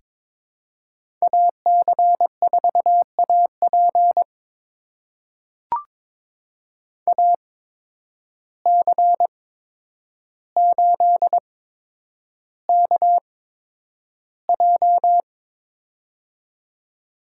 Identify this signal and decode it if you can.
Morse code, heard as a keyed tone, about 22 wpm, spaced as for 9 wpm, 700 Hz. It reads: AC4AP AC8KJ
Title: USA Call Signs - 5 Characters - 8x Character Spacing - 22wpm